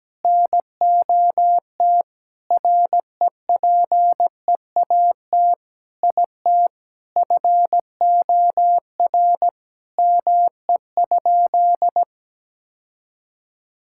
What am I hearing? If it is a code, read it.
Morse code, heard as a keyed tone, about 17 wpm, 700 Hz. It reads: NOT REPEAT IT FOR ME?
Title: Sentences from Top 1600 Words 17wpm